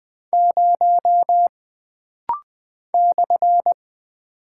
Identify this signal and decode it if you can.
Morse code, heard as a keyed tone, about 20 wpm, 700 Hz. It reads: 0 /
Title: Single Character (Letter:Number:Punctuation:Prosign) - Rapid-Fire - 20wpm